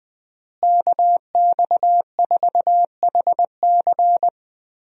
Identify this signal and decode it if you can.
Morse code, heard as a keyed tone, about 21 wpm, 700 Hz. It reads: KX4HC
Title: USA Call Signs - 5 Characters - Repeated 2x - 20wpm